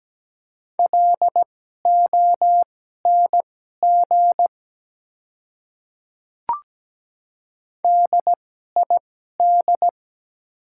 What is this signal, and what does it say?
Morse code, heard as a keyed tone, about 17 wpm, spaced as for 12 wpm, 700 Hz. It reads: LONG DID
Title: 2x Character Spacing - Top 100 Words - 17wpm